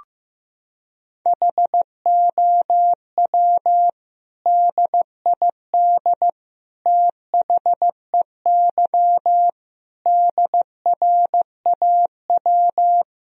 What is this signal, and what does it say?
Morse code, heard as a keyed tone, about 16 wpm, 700 Hz. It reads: HOW DID THEY DRAW
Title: Sentences from Top 1000 Words - Review Part 1 - 15wpm